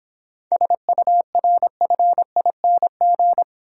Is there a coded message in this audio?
Yes: Morse code from 0.5 to 3.4 s, heard as a keyed tone, about 26 wpm, 700 Hz. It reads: SURFING